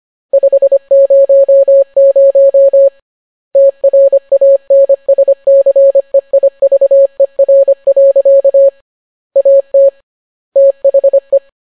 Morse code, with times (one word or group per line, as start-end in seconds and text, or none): 0.3-2.9 500
3.6-8.7 TRANSCEIVER.
9.4-9.9 AT
10.6-11.4 THE